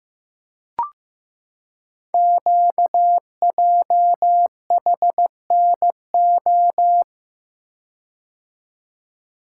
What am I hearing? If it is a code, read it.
Morse code, heard as a keyed tone, about 15 wpm, 700 Hz. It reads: QJHNO
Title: Code Groups - 5 Letters - 15wpm